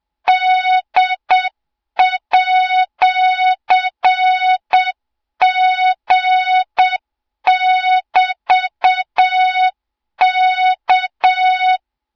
Morse code, heard as a keyed tone, about 7 wpm, 3700 Hz, with noise. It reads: D@G<BT>K
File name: Morze-HA2MN / hang / 20 GYAKORLAT